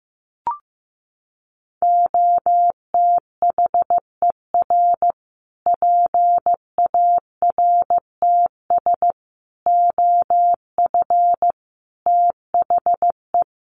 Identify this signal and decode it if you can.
Morse code, heard as a keyed tone, about 15 wpm, 700 Hz. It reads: OTHER PARTS OF THE
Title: Sets of 5 Words Encore 15wpm